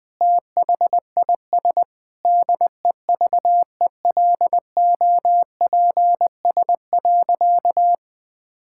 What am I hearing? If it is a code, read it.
Morse code, heard as a keyed tone, about 20 wpm, 700 Hz. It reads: THIS DEVELOPS.
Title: Sentences with 7-Word Phrases - Spoken Beginning - 20wpm